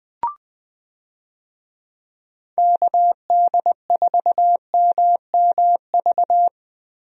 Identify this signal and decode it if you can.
Morse code, heard as a keyed tone, about 20 wpm, 700 Hz. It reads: KD4MMV
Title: USA Call Signs - 6 Characters - Repeated 3x - 20wpm